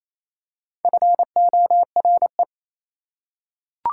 Morse code, heard as a keyed tone, about 28 wpm, 700 Hz. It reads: FORE
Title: Prefixes and Suffixes 28wpm